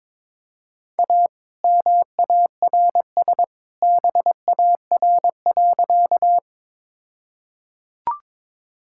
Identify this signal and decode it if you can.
Morse code, heard as a keyed tone, about 22 wpm, 700 Hz. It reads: A MARS BAR.
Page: morse-code-ninja-practice-22wpm